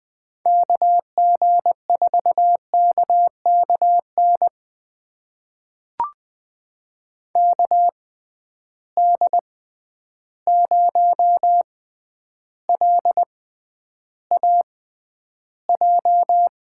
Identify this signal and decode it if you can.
Morse code, heard as a keyed tone, about 20 wpm, spaced as for 10 wpm, 700 Hz. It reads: KG4KKN KD0LAJ
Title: USA Call Signs - 6 Characters - 6x Character Spacing - 20wpm